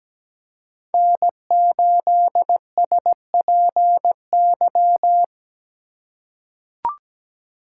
Morse code, heard as a keyed tone, about 17 wpm, 700 Hz. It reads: N8SPY
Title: USA Call Signs - 5 Characters - 4x Character Spacing - 17wpm